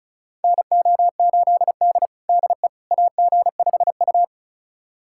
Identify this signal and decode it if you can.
Morse code, heard as a keyed tone, about 36 wpm, 700 Hz. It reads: NO8D DE AG5U
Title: USA Callsign DE Callsign - 4 Characters - 35wpm